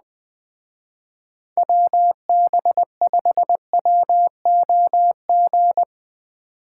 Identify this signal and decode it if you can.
Morse code, heard as a keyed tone, about 20 wpm, 700 Hz. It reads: WB5WOG
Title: Speed Racing - USA 2x3 Call Signs 17wpm